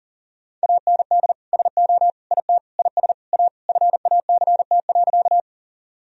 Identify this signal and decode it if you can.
Morse code, heard as a keyed tone, about 40 wpm, 700 Hz. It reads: AND SO IT IS A FACT.
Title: Sentences from Top 1000 Words - Review Part 3 - 40wpm